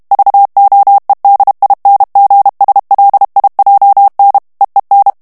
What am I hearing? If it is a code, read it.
Morse code, heard as a keyed tone, about 32 wpm, 800 Hz. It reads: VOEDINGSLIJN EEN
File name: morse mp3 / HOLLANDAIS texte clair 1 vitesse 30 mots minute